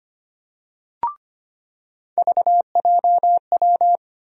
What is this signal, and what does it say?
Morse code, heard as a keyed tone, about 25 wpm, 700 Hz. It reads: VJW
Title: USA Call Sign Postfixes - 3 Letters 25wpm